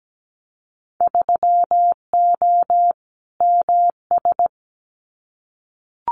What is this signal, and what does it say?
Morse code, heard as a keyed tone, about 17 wpm, 700 Hz. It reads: 3O MS
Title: Winter Field Day Exchanges Encore 17wpm